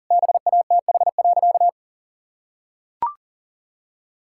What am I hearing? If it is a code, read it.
Morse code, heard as a keyed tone, about 40 wpm, 700 Hz. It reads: BATH.
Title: Sentences from Top 1600 Words 40wpm